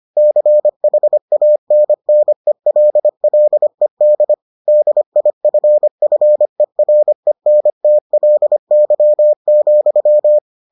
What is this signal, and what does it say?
Morse code, heard as a keyed tone, about 25 wpm, 600 Hz. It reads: CHANNELLED DIFFERENTLY,